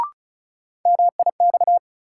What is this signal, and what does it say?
Morse code, heard as a keyed tone, about 36 wpm, 700 Hz. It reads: MIX
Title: Sentences from Top 1100 Words - New Word Drill Alt - 35wpm